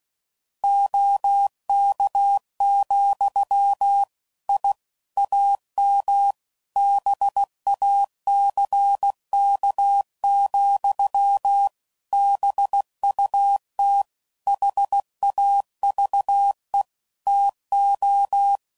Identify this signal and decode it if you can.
Morse code, heard as a keyed tone, about 16 wpm, 800 Hz. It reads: OK, I AM BACK, BUT HAVE TO
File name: CW Code Groups @15WPM / QSOs / WA7QOX QRX